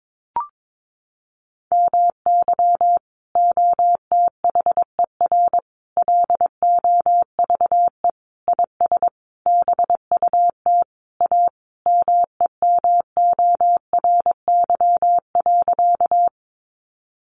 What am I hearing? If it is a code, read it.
Morse code, heard as a keyed tone, about 22 wpm, 700 Hz. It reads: MY OTHER LOVE IS BUT A MEMORY.
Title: Sentences from Top 1300 Words 22wpm